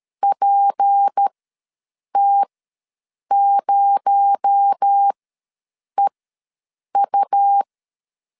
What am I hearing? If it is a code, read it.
Morse code, heard as a keyed tone, about 13 wpm, spaced as for 7 wpm, 800 Hz. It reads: PT0EU
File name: LEZIONI / 10 SIGNS / 10D COMA ws 12wpm 3618